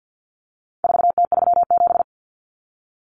Morse code, heard as a keyed tone, about 50 wpm, 700 Hz. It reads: 4T37
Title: Common Cut Numbers - 4-Digit 50wpm